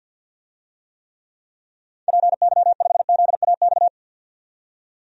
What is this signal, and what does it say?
Morse code, heard as a keyed tone, about 49 wpm, 700 Hz. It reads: PY5ZAX
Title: DXCC-100 Call Signs - 6 Characters - 50wpm